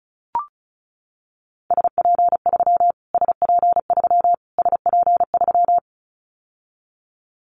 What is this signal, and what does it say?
Morse code, heard as a keyed tone, about 34 wpm, 700 Hz. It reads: SP3 SP3 SP3
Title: Callsign Prefixes for Top 25 DXCC - Repeated 3x - 35wpm